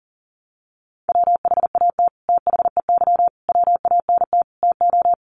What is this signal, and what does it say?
Morse code, heard as a keyed tone, about 40 wpm, 700 Hz. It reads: WHAT THEY WANT TO
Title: Sets of 5 Words Encore2 40wpm